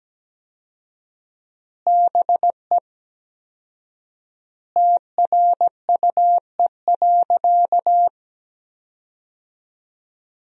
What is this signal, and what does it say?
Morse code, heard as a keyed tone, about 17 wpm, 700 Hz. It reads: BE TRUE.
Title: 4x Word-Spacing - Sentences from Top 400 Words - 17wpm